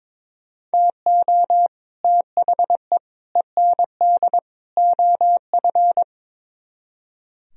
Morse code, heard as a keyed tone, about 22 wpm, 700 Hz. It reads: TO THE END OF
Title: Speed-Racing - Sets of 4 Words 22wpm